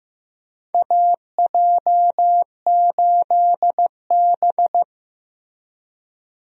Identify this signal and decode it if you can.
Morse code, heard as a keyed tone, about 15 wpm, 700 Hz. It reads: AJ8B
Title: USA Call Signs - 4 Characters - Repeated 2x - 15wpm